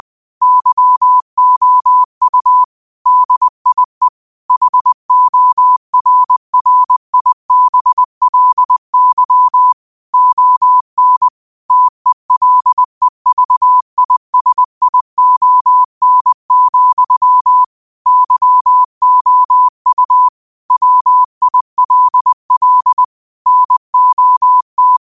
Morse code, heard as a keyed tone, about 20 wpm, 1000 Hz. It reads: YOU DIE HORRIBLY ON TELEVISION, YOU WILL NOT
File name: qotdfiles / 20wpm